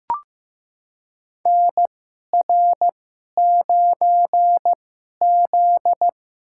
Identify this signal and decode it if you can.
Morse code, heard as a keyed tone, about 15 wpm, spaced as for 11 wpm, 700 Hz. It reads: NR9Z
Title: USA Call Signs - 4 Characters - 2x Character Spacing - 15wpm